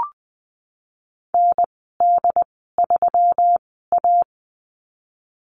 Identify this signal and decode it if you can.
Morse code, heard as a keyed tone, about 20 wpm, spaced as for 15 wpm, 700 Hz. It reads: ND3A
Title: 179 Lesson 29g - Slash - Call Signs 20wpm